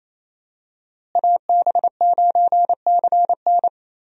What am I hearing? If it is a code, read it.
Morse code, heard as a keyed tone, about 28 wpm, 700 Hz. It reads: AB9CN
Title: USA Call Signs - 5 Characters - Repeated 3x - 28wpm